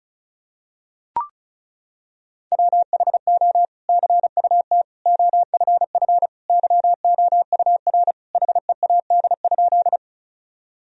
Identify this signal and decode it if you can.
Morse code, heard as a keyed tone, about 36 wpm, 700 Hz. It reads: WHO CUT OFF YOUR HEAD?
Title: Sentences from Top 1000 Words - Review Part 1 - 35wpm